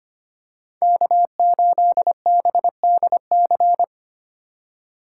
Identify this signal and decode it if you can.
Morse code, heard as a keyed tone, about 25 wpm, 700 Hz. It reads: K8BDC
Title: USA Call Signs - 5 Characters - 4x Character Spacing - 25wpm